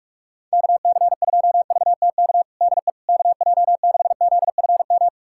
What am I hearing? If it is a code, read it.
Morse code, heard as a keyed tone, about 46 wpm, 700 Hz. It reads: KC2VTK DE KJ6ZFM
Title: USA Callsign DE Callsign 45wpm